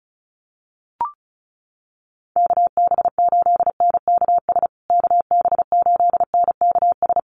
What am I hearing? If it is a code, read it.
Morse code, heard as a keyed tone, about 34 wpm, 700 Hz. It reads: KB8NKS KB8NKS
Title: USA Call Signs Repeated 3x 35wpm